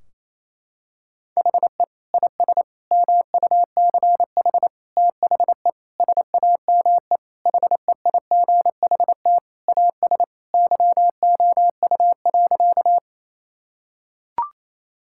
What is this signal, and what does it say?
Morse code, heard as a keyed tone, about 28 wpm, 700 Hz. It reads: HE IS MUCH THE SAME HEIGHT AS YOU.
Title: Sentences from Top 1600 Words 28wpm